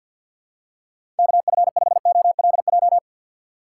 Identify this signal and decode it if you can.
Morse code, heard as a keyed tone, about 53 wpm, 700 Hz. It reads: KF5OLJ
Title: USA Call Signs - 6 Characters - 8x Character Spacing - 50wpm